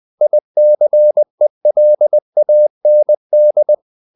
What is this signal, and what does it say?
Morse code, heard as a keyed tone, about 20 wpm, 600 Hz. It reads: ICELAND
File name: CW / Practice / NationalCapitals20